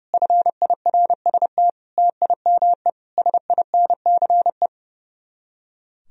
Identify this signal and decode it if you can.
Morse code, heard as a keyed tone, about 37 wpm, 700 Hz. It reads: FIRST TIME SINCE
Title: Speed-Racing - Sets of 4 Words 30wpm